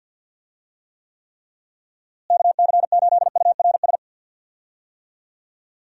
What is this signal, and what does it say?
Morse code, heard as a keyed tone, about 52 wpm, 700 Hz. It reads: KC8URS
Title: USA Call Signs - 6 Characters - Repeated 3x - 50wpm